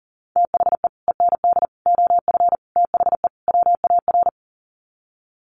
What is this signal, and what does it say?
Morse code, heard as a keyed tone, about 40 wpm, 700 Hz. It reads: THE END OF THE WAR
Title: Sets of 5 Words 40wpm